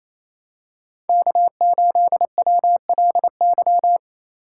Morse code, heard as a keyed tone, about 28 wpm, 700 Hz. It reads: K8WLY
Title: USA Call Signs - 5 Characters - 2x Character Spacing - 28wpm